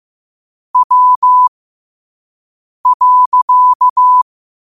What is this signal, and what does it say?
Morse code, heard as a keyed tone, about 15 wpm, spaced as for 5 wpm, 1000 Hz. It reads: W.
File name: qotdfiles / 5wpm